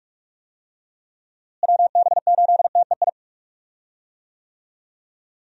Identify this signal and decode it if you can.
Morse code, heard as a keyed tone, about 46 wpm, 700 Hz. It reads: WB8TEI
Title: USA Call Signs - 6 Characters - Repeated 2x - 45wpm